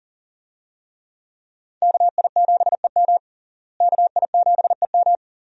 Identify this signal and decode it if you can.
Morse code, heard as a keyed tone, about 40 wpm, 700 Hz. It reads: KI7EM KI7EM
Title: USA Call Signs - 5 Characters - Repeated 2x - 40wpm